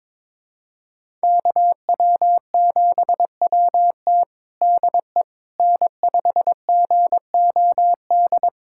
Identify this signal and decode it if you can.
Morse code, heard as a keyed tone, about 22 wpm, 700 Hz. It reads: KW7WT DE N5GOD
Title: USA Callsign DE Callsign - 5 Characters - 22wpm